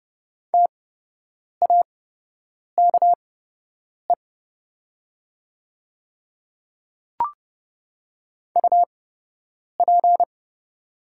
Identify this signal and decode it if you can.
Morse code, heard as a keyed tone, about 30 wpm, spaced as for 8 wpm, 700 Hz. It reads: TAKE UP
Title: 8x Character Spacing - Top 300 Words - 30wpm